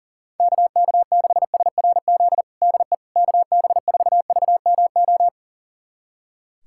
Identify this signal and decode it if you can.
Morse code, heard as a keyed tone, about 40 wpm, 700 Hz. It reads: KK6SRZ DE KB4VMO
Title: USA Callsign DE Callsign - 6 Characters - 40wpm